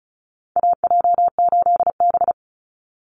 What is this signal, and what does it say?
Morse code, heard as a keyed tone, about 34 wpm, 700 Hz. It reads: AJ8B
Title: Call Signs - 35wpm